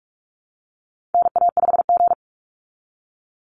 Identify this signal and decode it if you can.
Morse code, heard as a keyed tone, about 44 wpm, 700 Hz. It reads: NA5G
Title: Call Signs Repeated 3x - 45wpm